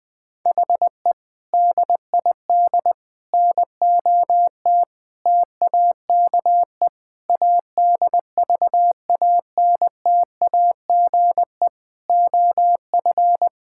Morse code, heard as a keyed tone, about 21 wpm, 700 Hz. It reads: HE DID NOT TAKE ADVANTAGE OF
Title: Sentences from Top 1000 Words - Review Part 6 - 20wpm